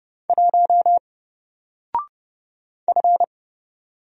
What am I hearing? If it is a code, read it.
Morse code, heard as a keyed tone, about 30 wpm, 700 Hz. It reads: 1 F
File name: Single-Letter-Number-30wpm